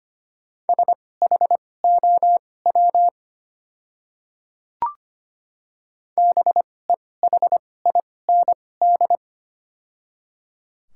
Morse code, heard as a keyed tone, about 25 wpm, spaced as for 18 wpm, 700 Hz. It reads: SHOW BEHIND
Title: 2x Character Spacing - Top 500 Words - 25wpm